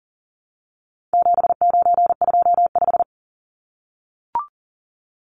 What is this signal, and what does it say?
Morse code, heard as a keyed tone, about 38 wpm, 700 Hz. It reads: 7925
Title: Numbers - 4-Digit 40wpm